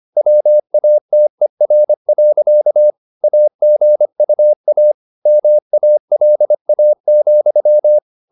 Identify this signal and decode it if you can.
Morse code, heard as a keyed tone, about 25 wpm, 600 Hz. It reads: WATER. AGUA MALA,
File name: the old man and the sea part 1 25wpm